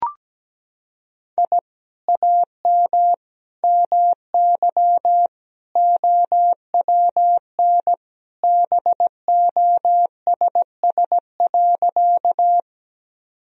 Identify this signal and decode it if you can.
Morse code, heard as a keyed tone, about 17 wpm, 700 Hz. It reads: I AM MY OWN BOSS.
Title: Sentences from Top 1000 Words - Review Part 4 - 17wpm